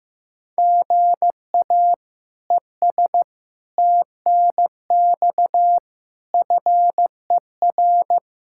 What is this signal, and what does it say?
Morse code, heard as a keyed tone, about 15 wpm, 700 Hz. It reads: GA ES TNX FER
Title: Practice QSO - Exchange 1 - Beginning - 15wpm